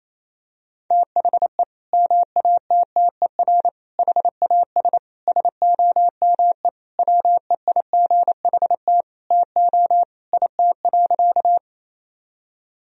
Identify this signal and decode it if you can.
Morse code, heard as a keyed tone, about 28 wpm, 700 Hz. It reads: THE MATTER HAS SOME WEIGHT TO IT.
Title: Sentences from Top 1000 Words - Review Part 3 - 28wpm